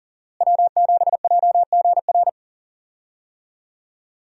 Morse code, heard as a keyed tone, about 40 wpm, 700 Hz. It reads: W7JGR
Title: USA Call Signs - 5 Characters - Repeated 3x - 40wpm